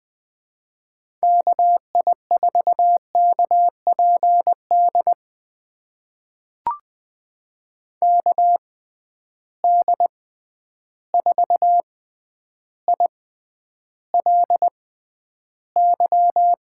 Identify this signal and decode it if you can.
Morse code, heard as a keyed tone, about 20 wpm, spaced as for 10 wpm, 700 Hz. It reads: KI4KPD KD4ILY
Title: USA Call Signs - 6 Characters - 6x Character Spacing - 20wpm